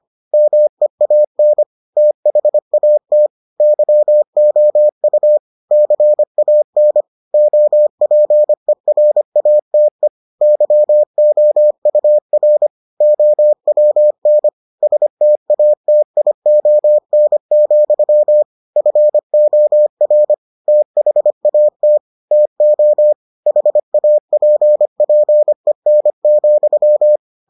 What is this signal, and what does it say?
Morse code, heard as a keyed tone, about 25 wpm, 600 Hz. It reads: MEAN THAT YOU CAN OPERATE YOUR OWN STATION, FOR THAT TO HAPPEN,